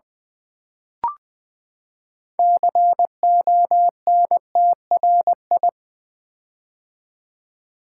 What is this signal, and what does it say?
Morse code, heard as a keyed tone, about 20 wpm, 700 Hz. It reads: CONTRI
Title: Build A Word - 6 Letters At A Time - 12-Letter Words - 20wpm